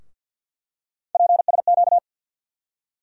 Morse code, heard as a keyed tone, about 47 wpm, 700 Hz. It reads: PSQ
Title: USA Call Sign Postfixes - 3 Letters 50wpm